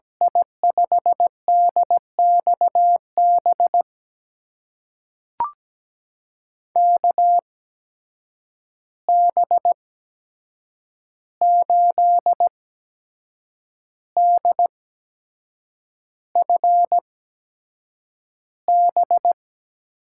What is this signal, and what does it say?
Morse code, heard as a keyed tone, about 17 wpm, spaced as for 7 wpm, 700 Hz. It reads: I5DXB KB8DFB
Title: USA Call Signs - 6 Characters - 8x Character Spacing - 17wpm